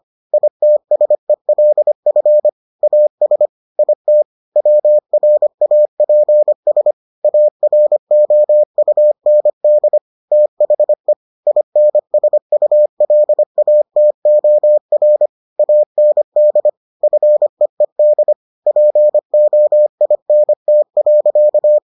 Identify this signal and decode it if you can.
Morse code, heard as a keyed tone, about 25 wpm, 600 Hz. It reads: ITSELF AS IT WRAPS AROUND THE INSULATOR AND FEED POINT.